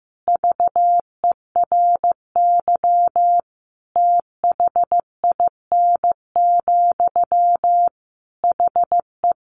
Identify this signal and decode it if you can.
Morse code, heard as a keyed tone, about 15 wpm, 700 Hz. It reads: VERY THIN, HE
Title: Sentences from Top 1200 Words 15wpm